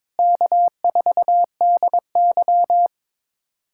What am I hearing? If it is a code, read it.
Morse code, heard as a keyed tone, about 22 wpm, 700 Hz. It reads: K4DY
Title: USA Callsign DE Callsign - 4 Characters - 22wpm